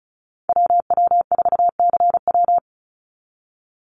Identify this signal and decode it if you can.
Morse code, heard as a keyed tone, about 34 wpm, 700 Hz. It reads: WW4CW
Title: USA Call Signs 35wpm